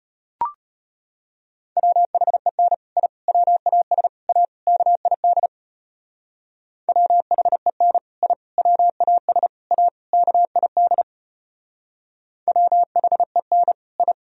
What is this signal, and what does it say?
Morse code, heard as a keyed tone, about 31 wpm, 700 Hz. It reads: WHEN I WAS A KID WHEN I WAS A KID WHEN I
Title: Speed-Racing - Sets of 5 Words 25wpm